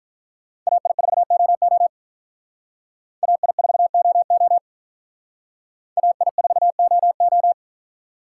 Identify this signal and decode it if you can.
Morse code, heard as a keyed tone, about 44 wpm, 700 Hz. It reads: AI4OO AI4OO AI4OO